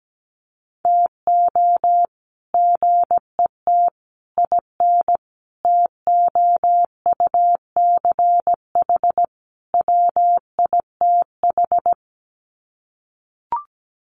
Morse code, heard as a keyed tone, about 17 wpm, 700 Hz. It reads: TO GET IN TOUCH WITH
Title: Sets of 5 Words Encore2 17wpm